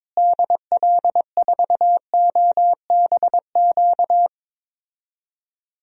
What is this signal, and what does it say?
Morse code, heard as a keyed tone, about 22 wpm, 700 Hz. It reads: DL4OBQ
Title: USA Call Signs - 6 Characters - Repeated 3x - 22wpm